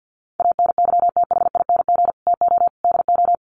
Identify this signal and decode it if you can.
Morse code, heard as a keyed tone, about 48 wpm, 700 Hz. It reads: ANYTHING TO DO
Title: Sets of 4 Words 50wpm